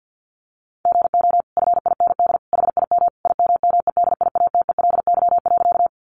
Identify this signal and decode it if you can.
Morse code, heard as a keyed tone, about 50 wpm, 700 Hz. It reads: GO FIND HIM IMMEDIATELY.
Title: Sentences from Top 1100 Words 50wpm